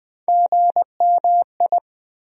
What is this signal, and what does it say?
Morse code, heard as a keyed tone, about 20 wpm, 700 Hz. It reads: GMI
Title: USA Call Sign Postfixes - 3 Letters 20wpm